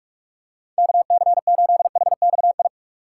Morse code, heard as a keyed tone, about 41 wpm, 700 Hz. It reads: KC8HXI
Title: USA Call Signs - 6 Characters - 6x Character Spacing - 45wpm